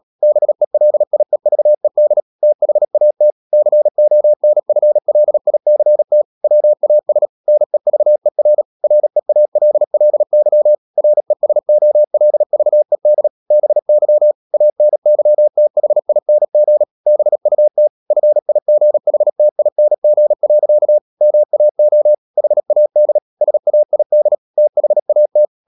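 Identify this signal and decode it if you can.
Morse code, heard as a keyed tone, about 37 wpm, 600 Hz. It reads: BELIEVED THAT CONFLICT WAS NEVER REALLY RESOLVED BY ANYTHING BUT FIGHTING. MAO HAD SAID THAT
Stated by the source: Morse code practice at 35–39 wpm sending English text of Ken Follett's novel Never, from CW med Gnister.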